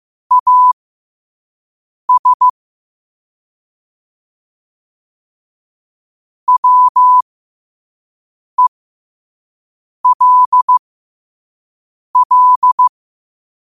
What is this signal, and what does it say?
Morse code, heard as a keyed tone, about 15 wpm, spaced as for 5 wpm, 1000 Hz. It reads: AS WELL
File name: qotdfiles / 5wpm